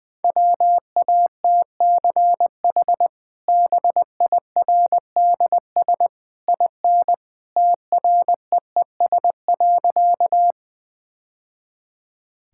Morse code, heard as a keyed tone, about 20 wpm, 700 Hz. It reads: WATCH BIRDS IN TREES.
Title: Easy Sentences with Hints - 20wpm